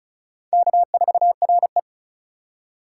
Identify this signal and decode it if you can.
Morse code, heard as a keyed tone, about 34 wpm, 690 Hz. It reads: K4RE
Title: USA Call Signs - 4 Characters - 2x Character Spacing - 35wpm